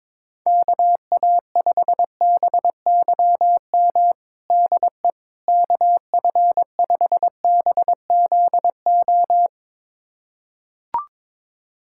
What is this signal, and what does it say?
Morse code, heard as a keyed tone, about 22 wpm, 700 Hz. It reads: KA5BYM DE KF5BZO
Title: USA Callsign DE Callsign 22wpm